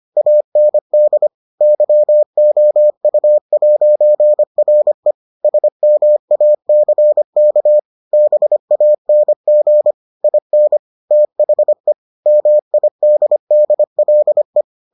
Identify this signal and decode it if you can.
Morse code, heard as a keyed tone, about 25 wpm, 600 Hz. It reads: AND YOU'RE SMACK BANG IN THE MIDDLE